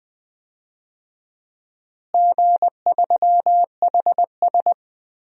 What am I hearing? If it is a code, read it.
Morse code, heard as a keyed tone, about 20 wpm, 700 Hz. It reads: G3HS
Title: DXCC-100 Call Signs - 4 Characters - Repeated 3x - 20wpm